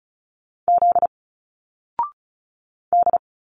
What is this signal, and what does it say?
Morse code, heard as a keyed tone, about 34 wpm, 700 Hz. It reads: Z D
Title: Single Letters 35wpm